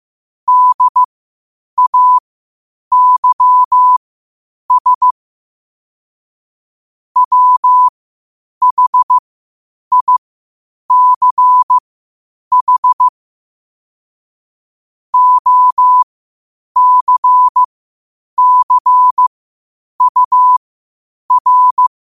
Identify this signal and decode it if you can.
Morse code, heard as a keyed tone, about 15 wpm, spaced as for 8 wpm, 1000 Hz. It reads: DAYS WHICH OCCUR
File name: qotdfiles / 8wpm